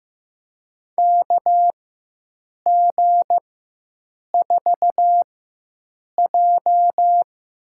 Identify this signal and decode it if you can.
Morse code, heard as a keyed tone, about 15 wpm, spaced as for 7 wpm, 700 Hz. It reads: KG4J
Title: USA Call Signs - 6 Characters - 4x Character Spacing - 15wpm